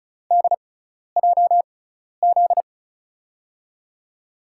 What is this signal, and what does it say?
Morse code, heard as a keyed tone, about 36 wpm, spaced as for 12 wpm, 690 Hz. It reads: DJZ
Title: USA Call Signs - 6 Characters - 6x Character Spacing - 35wpm